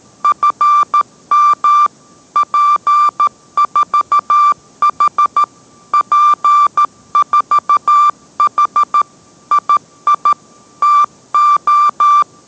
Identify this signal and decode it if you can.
Morse code, heard as a keyed tone, about 13 wpm, 1180 Hz. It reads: FM P4H P4H II TO